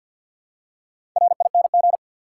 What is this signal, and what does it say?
Morse code, heard as a keyed tone, about 47 wpm, 700 Hz. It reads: RING
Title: Four Letter Words - 50wpm